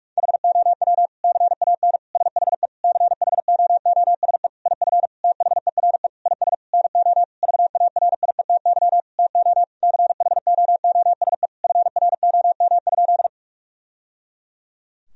Morse code, heard as a keyed tone, about 46 wpm, 700 Hz. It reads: HOW CAN SHE CHOOSE IF THERE IS NO VARIETY TO CHOOSE FROM?
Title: Sentences from Top 1000 Words - Review Part 4 - 45wpm